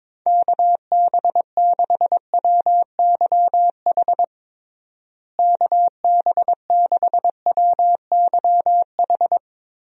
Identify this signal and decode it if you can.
Morse code, heard as a keyed tone, about 21 wpm, 700 Hz. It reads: KB6WYH KB6WYH
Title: USA Call Signs - 6 Characters - Repeated 3x - 22wpm